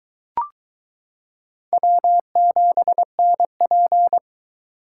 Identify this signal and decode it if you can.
Morse code, heard as a keyed tone, about 22 wpm, 700 Hz. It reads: W7NP